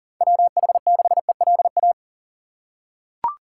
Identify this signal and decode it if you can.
Morse code, heard as a keyed tone, about 40 wpm, 700 Hz. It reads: WH6ELA
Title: USA Call Signs - 6 Characters - 3x Character Spacing - 40wpm